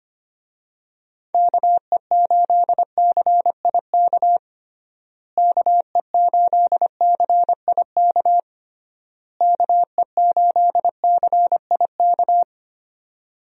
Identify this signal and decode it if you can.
Morse code, heard as a keyed tone, about 25 wpm, 700 Hz. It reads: KE8CIK KE8CIK KE8CIK
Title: USA Call Signs - 6 Characters - Repeated 3x - 25wpm